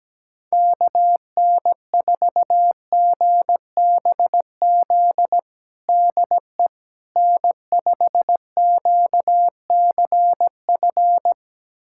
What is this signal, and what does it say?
Morse code, heard as a keyed tone, about 17 wpm, 700 Hz. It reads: KN4GBZ DE N5QCF
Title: USA Callsign DE Callsign 17wpm